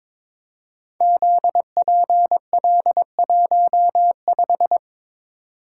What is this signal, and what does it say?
Morse code, heard as a keyed tone, about 22 wpm, 700 Hz. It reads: ZPL15
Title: Code Groups - 5 Letter:Numbers - 22wpm